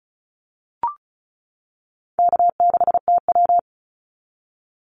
Morse code, heard as a keyed tone, about 34 wpm, 700 Hz. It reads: K6TW
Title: NA SOTA Activator Callsigns 35wpm